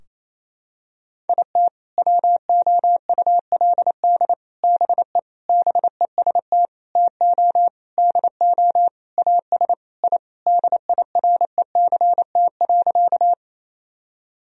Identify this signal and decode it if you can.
Morse code, heard as a keyed tone, about 27 wpm, 700 Hz. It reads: IT WOULD BE BEST TO DO AS I DIRECT.
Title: Sentences from Top 1000 Words - Review Part 2 - 28wpm